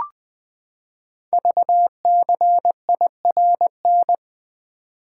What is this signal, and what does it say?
Morse code, heard as a keyed tone, about 20 wpm, 700 Hz. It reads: VCIRN
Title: Code Groups - 5 Letter:Numbers - 20wpm